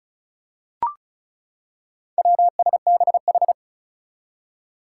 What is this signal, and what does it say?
Morse code, heard as a keyed tone, about 36 wpm, 700 Hz. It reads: WSBH
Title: Four Challenging Letter-Number Combinations 35wpm